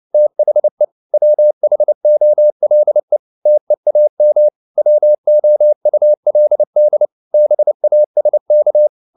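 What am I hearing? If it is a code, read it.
Morse code, heard as a keyed tone, about 29 wpm, 600 Hz. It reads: THE WHOLE TEAM WOULD BASK
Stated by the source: Morse code at 29 wpm, sending Ken Follett's novel Never in English, from CW med Gnister.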